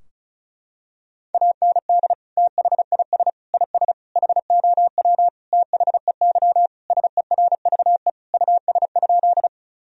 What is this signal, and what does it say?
Morse code, heard as a keyed tone, about 34 wpm, 700 Hz. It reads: AND THIS IS HOW THEY SERVE US?
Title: Sentences from Top 1000 Words - Review Part 2 - 35wpm